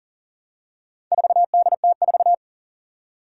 Morse code, heard as a keyed tone, about 40 wpm, 700 Hz. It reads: 4DT4
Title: Four Challenging Letter-Number Combinations 40wpm